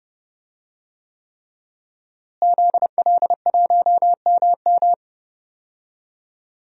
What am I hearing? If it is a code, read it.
Morse code, heard as a keyed tone, about 30 wpm, 700 Hz. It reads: ZL1MM
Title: DXCC-100 Call Signs - 5 Characters - Repeated 3x - 30wpm